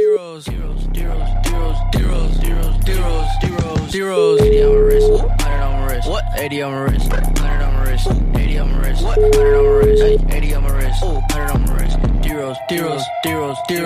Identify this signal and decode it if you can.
Ringback tone (Europe, 425 Hz), with speech.